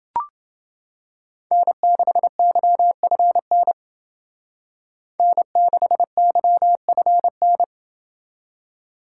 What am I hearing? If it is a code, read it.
Morse code, heard as a keyed tone, about 28 wpm, 700 Hz. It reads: N6YFN N6YFN